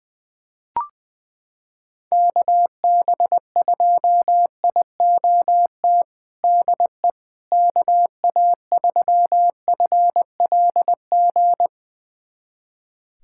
Morse code, heard as a keyed tone, about 20 wpm, 700 Hz. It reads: KB2IOT DE KA3FLG
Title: USA Callsign DE Callsign - 6 Characters - 20wpm